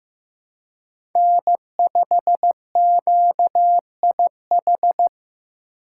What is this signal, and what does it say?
Morse code, heard as a keyed tone, about 15 wpm, 700 Hz. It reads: N5QIH
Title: USA Call Signs - 5 Characters - 8x Character Spacing - 15wpm